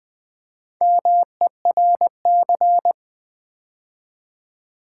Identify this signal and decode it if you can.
Morse code, heard as a keyed tone, about 20 wpm, 700 Hz. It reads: MERC
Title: Build A Word - 2 Letters At A Time - 8-Letter Words - 20wpm